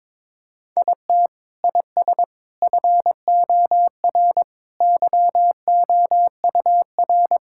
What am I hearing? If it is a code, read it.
Morse code, heard as a keyed tone, about 21 wpm, 700 Hz. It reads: IT IS FOR YOUR
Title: Sentences from Top 1600 Words 22wpm